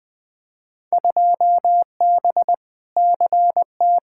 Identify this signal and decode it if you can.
Morse code, heard as a keyed tone, about 20 wpm, 700 Hz. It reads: 2B CT
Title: Field Day Exchanges Encore - 20wpm